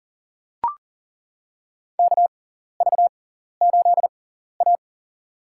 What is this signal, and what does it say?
Morse code, heard as a keyed tone, about 40 wpm, spaced as for 14 wpm, 700 Hz. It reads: KV8A
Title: USA Call Signs - 4 Characters - 6x Character Spacing - 40wpm